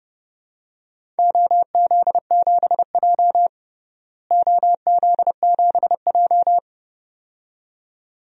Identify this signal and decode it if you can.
Morse code, heard as a keyed tone, about 31 wpm, 700 Hz. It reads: OZ7J OZ7J
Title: DXCC-100 Call Signs - 4 Characters - Repeated 2x - 30wpm